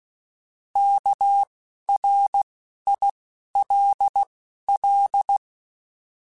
Morse code, heard as a keyed tone, about 16 wpm, 800 Hz. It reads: K R I L L
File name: Alphabetical / Alphabetical mp3s / Ks @10